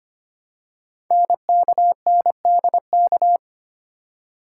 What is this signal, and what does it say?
Morse code, heard as a keyed tone, about 25 wpm, 700 Hz. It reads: NKNDK
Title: Code Groups - 5 Letters - 25wpm